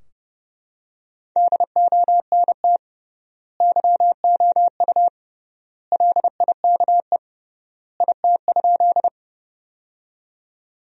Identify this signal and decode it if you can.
Morse code, heard as a keyed tone, about 30 wpm, 700 Hz. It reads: DONT YOU LIKE IT?
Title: 3x Word-Spacing - Sentences from Top 300 Words - 30wpm